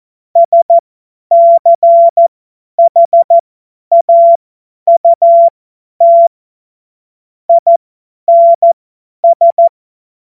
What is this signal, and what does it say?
Morse code, heard as a keyed tone, about 14 wpm, spaced as for 10 wpm, 680 Hz. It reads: SCHAUT INS
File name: Morsekurs / Witze / Witz 04. Oktober 2022 50 F70